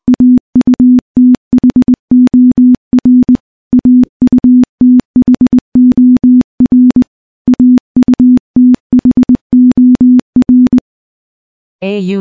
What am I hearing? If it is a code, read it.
Morse code, heard as a keyed tone, about 20 wpm, 250 Hz. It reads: AUTHOR AUTHOR AUTHOR
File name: morse-training-20WPM